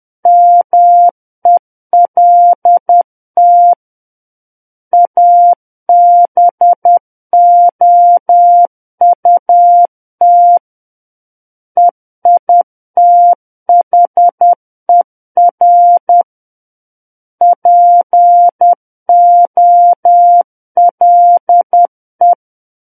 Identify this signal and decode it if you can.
Morse code, heard as a keyed tone, about 10 wpm, 700 Hz. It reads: MELT ABOUT EITHER POLE
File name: Book One / 01-Chapter 1 - 10 WPM